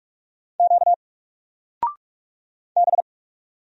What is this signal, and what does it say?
Morse code, heard as a keyed tone, about 44 wpm, 700 Hz. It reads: Q B
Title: Single Character (Letter:Number:Punctuation:Prosign) 45wpm